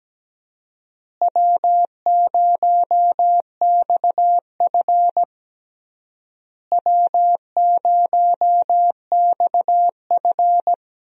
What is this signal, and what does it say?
Morse code, heard as a keyed tone, about 17 wpm, 700 Hz. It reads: W0XF W0XF
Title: USA Call Signs - 4 Characters - Repeated 3x - 17wpm